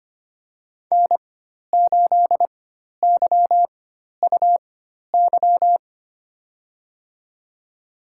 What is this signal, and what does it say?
Morse code, heard as a keyed tone, about 25 wpm, spaced as for 12 wpm, 700 Hz. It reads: N8YUY
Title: USA Call Signs - 5 Characters - 4x Character Spacing - 25wpm